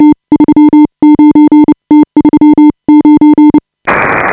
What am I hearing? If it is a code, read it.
Morse code, heard as a keyed tone, about 30 wpm, spaced as for 25 wpm, 310 Hz. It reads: T39 T39
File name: AUDIO2018 / HA5BA CW